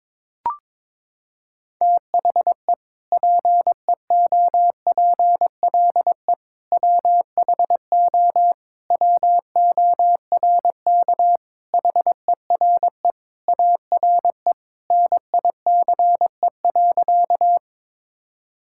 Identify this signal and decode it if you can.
Morse code, heard as a keyed tone, about 22 wpm, 700 Hz. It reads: THE PEOPLE WHO WORK HERE ARE NICE.
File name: practice / 22wpm / Sentences with 3-Word Phrases - 22wpm